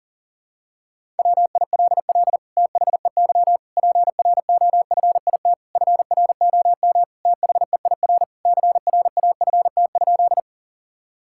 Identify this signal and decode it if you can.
Morse code, heard as a keyed tone, about 40 wpm, 700 Hz. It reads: WILL THEY PROFIT FROM THEIR CRAFT?
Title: Sentences from Top 1000 Words - Review Part 6 - 40wpm